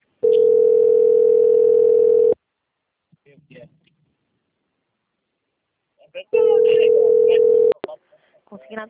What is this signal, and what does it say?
Ringback tone (North America), 440 + 480 Hz, with speech.
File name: out- - - - - .9646